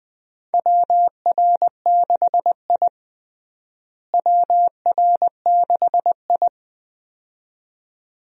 Morse code, heard as a keyed tone, about 21 wpm, 700 Hz. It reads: WR6I WR6I
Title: USA Call Signs - 4 Characters - Repeated 2x - 20wpm